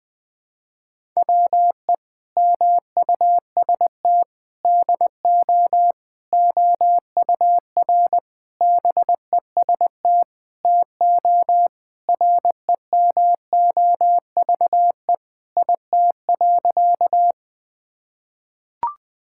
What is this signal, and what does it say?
Morse code, heard as a keyed tone, about 21 wpm, 700 Hz. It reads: WE MUST DO OUR BEST TO REMOVE IT.
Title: Sentences from Top 1000 Words - Review Part 5 - 20wpm